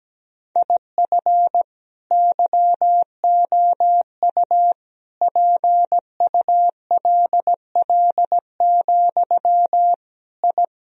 Morse code, heard as a keyed tone, about 17 wpm, 700 Hz. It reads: IF YOU PULL, I
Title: Sentences from Top 1000 Words - Review Part 2 - 17wpm